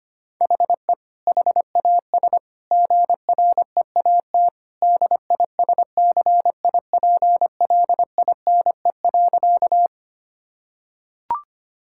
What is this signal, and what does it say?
Morse code, heard as a keyed tone, about 25 wpm, 700 Hz. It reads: HE HAS GREAT DISCIPLINE.
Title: Sentences from Top 1000 Words - Review Part 6 - 25wpm